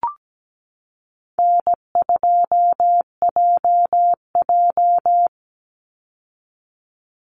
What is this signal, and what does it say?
Morse code, heard as a keyed tone, about 17 wpm, 700 Hz. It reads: N2JJ
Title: Easy Call Signs Repeated 3x - 17wpm